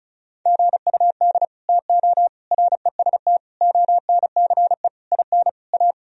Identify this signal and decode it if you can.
Morse code, heard as a keyed tone, about 36 wpm, 700 Hz. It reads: GUD TO REST ONCE IN A
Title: Sentences from Top 500 Words and Abbreviations 35wpm